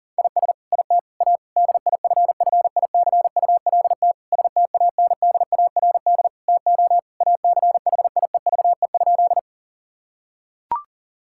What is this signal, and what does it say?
Morse code, heard as a keyed tone, about 40 wpm, 700 Hz. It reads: IS IT A DIFFICULT STANDARD TO ACHIEVE?
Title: Sentences from Top 1000 Words - Review Part 6 - 40wpm